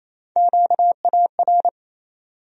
Morse code, heard as a keyed tone, about 28 wpm, 700 Hz. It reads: QAR
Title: USA Call Sign Postfixes - 3 Letters 28wpm